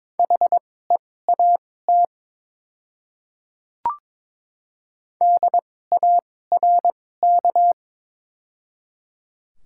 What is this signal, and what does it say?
Morse code, heard as a keyed tone, about 22 wpm, spaced as for 16 wpm, 700 Hz. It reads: HEAT DARK
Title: 2x Character Spacing - Top 500 Words - 22wpm